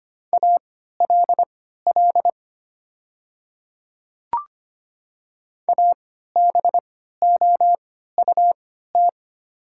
Morse code, heard as a keyed tone, about 25 wpm, spaced as for 14 wpm, 700 Hz. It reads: ALL ABOUT
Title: 3x Character Spacing - Top 100 Words - 25wpm